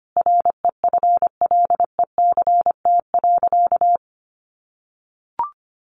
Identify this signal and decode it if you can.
Morse code, heard as a keyed tone, about 25 wpm, 700 Hz. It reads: REFLECT.
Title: Sentences from Top 1500 Words 25wpm